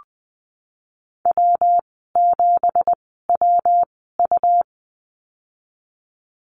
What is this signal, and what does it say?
Morse code, heard as a keyed tone, about 20 wpm, spaced as for 15 wpm, 700 Hz. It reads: W7WU
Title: 167 Lesson 27g - Number 7 - Call Signs 20wpm